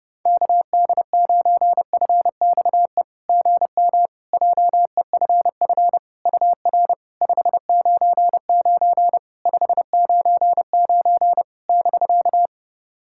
Morse code, heard as a keyed tone, about 30 wpm, 690 Hz. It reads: KD9FXE GM JEFF UR 599 599 <BK>
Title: SOTA QSO Practice 30wpm